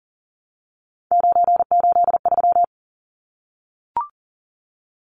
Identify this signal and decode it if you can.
Morse code, heard as a keyed tone, about 40 wpm, 700 Hz. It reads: 983
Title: Numbers - 3-Digit 40wpm